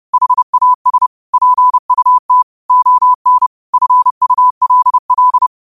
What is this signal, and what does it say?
Morse code, heard as a keyed tone, about 30 wpm, 1000 Hz. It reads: HAS PUT ON FULL
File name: qotdfiles / 30wpm